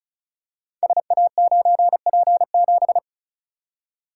Morse code, heard as a keyed tone, about 36 wpm, 690 Hz. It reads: SA9P7
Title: Code Groups - 5 Letter:Numbers - 35wpm